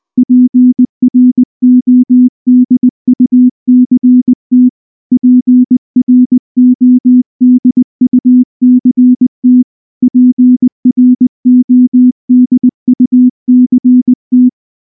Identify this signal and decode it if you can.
Morse code, heard as a keyed tone, about 19 wpm, 250 Hz. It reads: PRODUCT PRODUCT PRODUCT
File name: morse / morse-training-20WPM-1